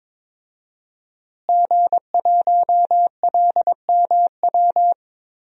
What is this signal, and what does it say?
Morse code, heard as a keyed tone, about 21 wpm, 700 Hz. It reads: G1LMW
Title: DXCC-100 Call Signs - 5 Characters - Repeated 2x - 22wpm